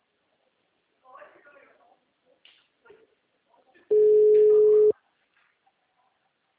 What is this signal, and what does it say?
Ringback tone (Europe, 425 Hz), with speech.
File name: out- -2041- - - .218